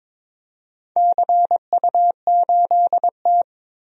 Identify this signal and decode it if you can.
Morse code, heard as a keyed tone, about 22 wpm, 700 Hz. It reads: CU8T
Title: DXCC-100 Call Signs - 4 Characters - 22wpm